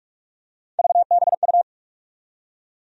Morse code, heard as a keyed tone, about 44 wpm, 700 Hz. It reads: VBU
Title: USA Call Sign Postfixes - 3 Letters 45wpm